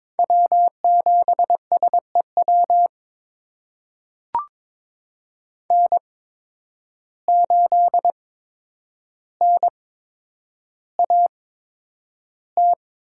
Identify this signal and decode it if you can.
Morse code, heard as a keyed tone, about 22 wpm, spaced as for 9 wpm, 700 Hz. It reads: W7SEW N8NAT
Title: USA Call Signs - 5 Characters - 8x Character Spacing - 22wpm